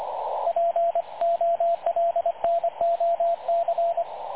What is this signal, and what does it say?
Morse code, heard as a keyed tone, about 25 wpm, 680 Hz, with radio noise. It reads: POLNOC
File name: dlazaawansowanych / nagraniazpasma / websdr recording 3567.0kHz